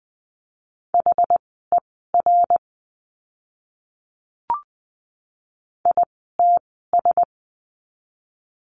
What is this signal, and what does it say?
Morse code, heard as a keyed tone, about 20 wpm, spaced as for 15 wpm, 700 Hz. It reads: HER ITS
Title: 230 Lesson 37f - Letter x - Words 20wpm